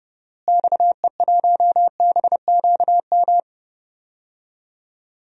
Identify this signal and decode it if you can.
Morse code, heard as a keyed tone, about 31 wpm, 700 Hz. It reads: XE1BQM
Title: DXCC-100 Call Signs - 6 Characters - Repeated 2x - 30wpm